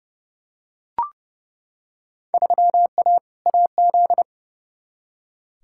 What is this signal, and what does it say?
Morse code, heard as a keyed tone, about 29 wpm, 700 Hz. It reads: 3A AZ
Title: Field Day Exchanges - 30wpm